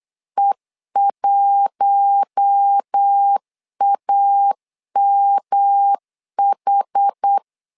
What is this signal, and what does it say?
Morse code, heard as a keyed tone, about 8 wpm, 800 Hz. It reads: E1AMH
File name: LEZIONI / 4 Q1 8wpm 3035